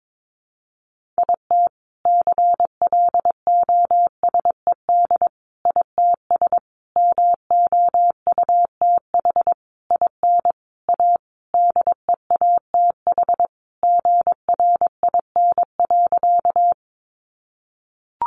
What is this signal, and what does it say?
Morse code, heard as a keyed tone, about 22 wpm, 700 Hz. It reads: IT CLOSED ITS MOUTH IN A DEATH GRIN.
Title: A Sound Of Thunder 22wpm